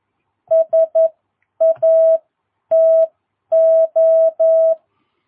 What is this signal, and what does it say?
Morse code, heard as a keyed tone, about 11 wpm, spaced as for 9 wpm, 660 Hz. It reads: SATO
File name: jh8cbh / 2022cw